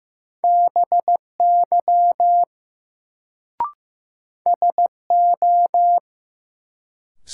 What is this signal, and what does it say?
Morse code, heard as a keyed tone, about 15 wpm, 700 Hz. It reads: BY SO
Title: Sentences from Top 100 Words - New Word Drill - 15wpm